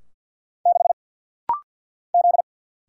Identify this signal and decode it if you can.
Morse code, heard as a keyed tone, about 50 wpm, 700 Hz. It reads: 6 Z
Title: Single Letter-Number Alt - Rapid-Fire 50wpm